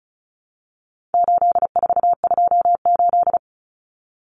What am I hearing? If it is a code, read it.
Morse code, heard as a keyed tone, about 34 wpm, 700 Hz. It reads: 8428
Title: Numbers - 4-Digit 35wpm